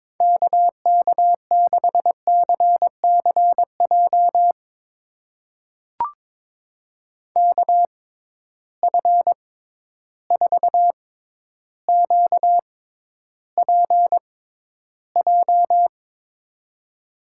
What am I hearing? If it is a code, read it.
Morse code, heard as a keyed tone, about 22 wpm, spaced as for 11 wpm, 700 Hz. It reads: KK6CCJ KF4QPJ
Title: USA Call Signs - 6 Characters - 6x Character Spacing - 22wpm